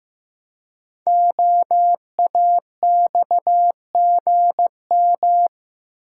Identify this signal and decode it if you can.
Morse code, heard as a keyed tone, about 15 wpm, 700 Hz. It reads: OAXGM
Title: Code Groups - 5 Letters - 15wpm